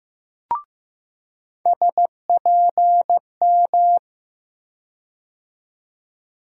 Morse code, heard as a keyed tone, about 15 wpm, 700 Hz. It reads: SPM
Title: USA Call Sign Postfixes - 3 Letters 15wpm